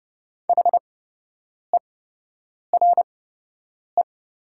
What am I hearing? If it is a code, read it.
Morse code, heard as a keyed tone, about 30 wpm, spaced as for 8 wpm, 700 Hz. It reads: HERE
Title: 8x Character Spacing - Top 300 Words - 30wpm